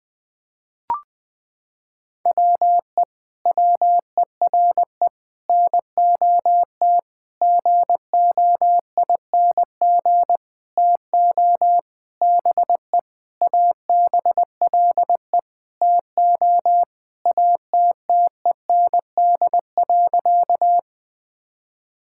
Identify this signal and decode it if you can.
Morse code, heard as a keyed tone, about 20 wpm, 700 Hz. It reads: WE WERE NOT GOING TO BE ABLE TO ATTEND.
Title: Sentences with 7-Word Phrases - 20wpm